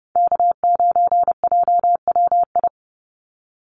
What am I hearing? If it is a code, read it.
Morse code, heard as a keyed tone, about 30 wpm, 700 Hz. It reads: K9JWI
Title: Call Signs Repeated 2x - 30wpm